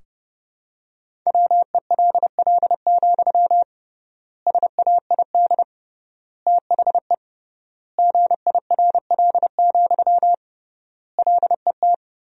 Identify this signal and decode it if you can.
Morse code, heard as a keyed tone, about 30 wpm, 700 Hz. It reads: WELL, SAID THE GIRL, LET
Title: 3x Word-Spacing - Sentences from Top 400 Words - 30wpm